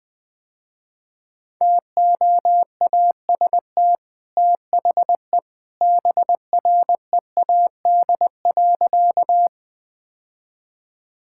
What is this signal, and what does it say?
Morse code, heard as a keyed tone, about 21 wpm, 700 Hz. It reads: TOAST THE BREAD.
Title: Easy Sentences with Hints - 20wpm